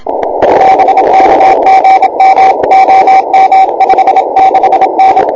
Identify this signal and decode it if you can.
Morse code, heard as a keyed tone, about 26 wpm, 770 Hz, with radio noise. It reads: BK GM OM 56N
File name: n6tr / K1NAandN6TRatHA5JI on 160M 09 09 26 AM0308 1